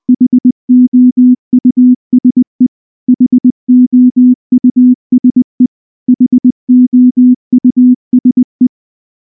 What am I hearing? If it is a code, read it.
Morse code, heard as a keyed tone, about 20 wpm, 250 Hz. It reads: HOUSE HOUSE HOUSE
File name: morse / morse-training-20WPM-6